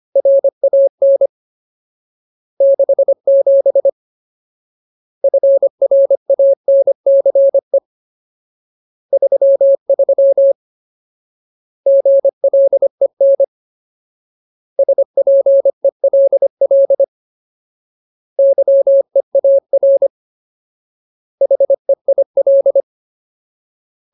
Morse code, heard as a keyed tone, about 25 wpm, 550 Hz. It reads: RAN 67 FRANCE 33 GLEN SPELL YEAR HEIL
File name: Ultimate Word List / 25WPM Ultimate